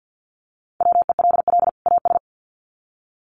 Morse code, heard as a keyed tone, about 50 wpm, 700 Hz. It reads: WELL AS
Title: Sets of 2 Words 50wpm